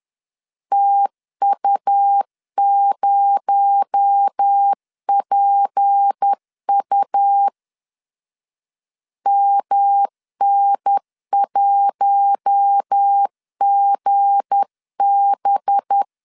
Morse code, heard as a keyed tone, about 10 wpm, 800 Hz. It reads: TU0PU MN1GB
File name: LEZIONI / 7 GF9Z / 7 G 10wpm 2501